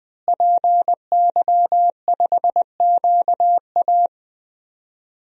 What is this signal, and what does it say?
Morse code, heard as a keyed tone, about 21 wpm, 700 Hz. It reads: PY5QA
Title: DXCC-100 Call Signs - 5 Characters - Repeated 3x - 20wpm